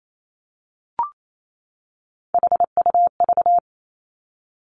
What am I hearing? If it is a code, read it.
Morse code, heard as a keyed tone, about 28 wpm, 690 Hz. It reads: HUV
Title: Three Challenging Letter-Number Combinations 28wpm